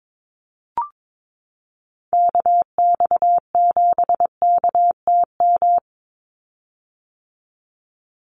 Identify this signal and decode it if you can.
Morse code, heard as a keyed tone, about 22 wpm, 700 Hz. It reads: KX7KTM
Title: USA Call Signs 22wpm